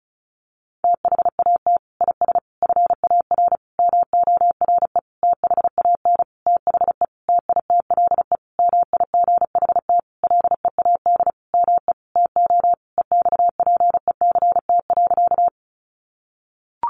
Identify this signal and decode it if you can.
Morse code, heard as a keyed tone, about 34 wpm, 700 Hz. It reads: THAT IS FAR MORE THAN THE TITLE MIGHT LEAD ME TO EXPECT.
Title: Sentences from Top 1200 Words 35wpm